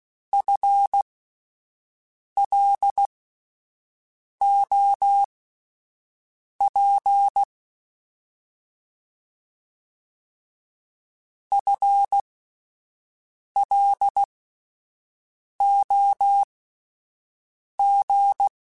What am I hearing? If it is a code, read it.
Morse code, heard as a keyed tone, about 16 wpm, spaced as for 5 wpm, 800 Hz. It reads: FLOP FLOG
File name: Alphabetical / Alphabetical mp3s / Fs @5